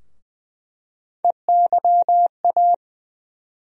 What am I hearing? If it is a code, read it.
Morse code, heard as a keyed tone, about 21 wpm, 700 Hz. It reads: EYA
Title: USA Call Sign Postfixes - 3 Letters 20wpm